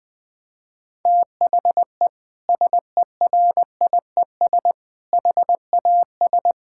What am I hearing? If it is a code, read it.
Morse code, heard as a keyed tone, about 20 wpm, 700 Hz. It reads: THE SERIES HAS
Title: Sentences from Top 1000 Words - Review Part 5 - 20wpm